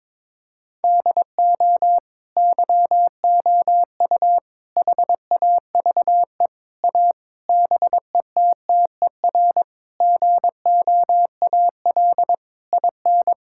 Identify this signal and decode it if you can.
Morse code, heard as a keyed tone, about 22 wpm, 700 Hz. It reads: DO YOU HAVE A BETTER GOAL IN
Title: Sentences from Top 1000 Words - Review Part 5 - 22wpm